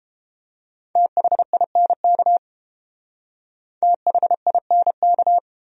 Morse code, heard as a keyed tone, about 29 wpm, 700 Hz. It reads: THINK THINK